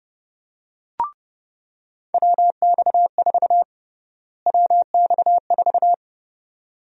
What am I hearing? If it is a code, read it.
Morse code, heard as a keyed tone, about 32 wpm, 700 Hz. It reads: WX4 WX4
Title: USA Call Sign Prefixes - Repeated 2x - 30wpm